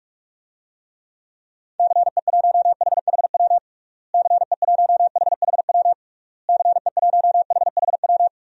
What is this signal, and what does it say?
Morse code, heard as a keyed tone, about 47 wpm, 700 Hz. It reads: CE1HHW CE1HHW CE1HHW
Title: DXCC-100 Call Signs - 6 Characters - Repeated 3x - 45wpm